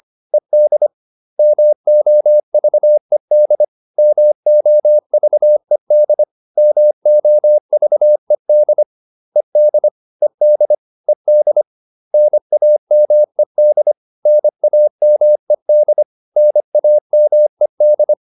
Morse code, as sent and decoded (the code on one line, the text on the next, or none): . -.. / -- --- ...- . -.. / -- --- ...- . -.. / -- --- ...- . -.. / . -.. / . -.. / . -.. / -. .- -- . -.. / -. .- -- . -.. / -. .- -- . -..
ED MOVED MOVED MOVED ED ED ED NAMED NAMED NAMED